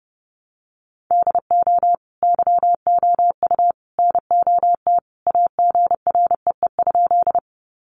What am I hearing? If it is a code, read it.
Morse code, heard as a keyed tone, about 30 wpm, 700 Hz. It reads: DO YOU NOT AGREE?
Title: Sentences from Top 1100 Words 30wpm